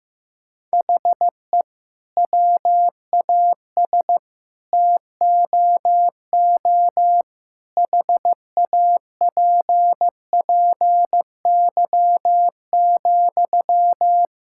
Morse code, heard as a keyed tone, about 16 wpm, 700 Hz. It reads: HE WAS TOO HAPPY,
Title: Sentences from Top 1000 Words - Review Part 6 - 15wpm